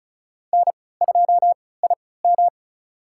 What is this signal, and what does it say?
Morse code, heard as a keyed tone, about 34 wpm, spaced as for 20 wpm, 700 Hz. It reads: N2IM
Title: USA Call Signs - 4 Characters - 3x Character Spacing - 35wpm